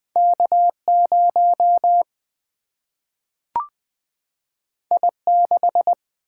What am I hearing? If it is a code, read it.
Morse code, heard as a keyed tone, about 20 wpm, 700 Hz. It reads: K 0 I 6
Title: Single-Letter Single-Number 20wpm